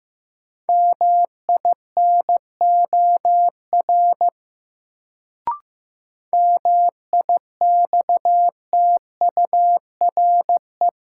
Morse code, heard as a keyed tone, about 15 wpm, 700 Hz. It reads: MINOR MIXTURE
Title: Sentences from Top 1500 Words - New Word Drill - 15wpm